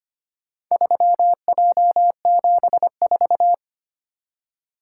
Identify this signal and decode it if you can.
Morse code, heard as a keyed tone, about 25 wpm, 700 Hz. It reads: 3J74
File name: practice / 25wpm / Four Letter-Number Combinations - 25wpm